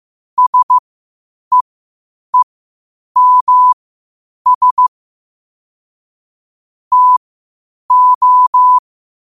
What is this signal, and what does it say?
Morse code, heard as a keyed tone, about 15 wpm, spaced as for 8 wpm, 1000 Hz. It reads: SEEMS TO
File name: qotdfiles / 8wpm